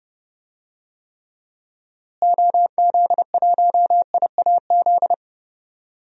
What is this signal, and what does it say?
Morse code, heard as a keyed tone, about 30 wpm, 700 Hz. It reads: OZ1IAZ
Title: DXCC-100 Call Signs - 6 Characters - Repeated 2x - 30wpm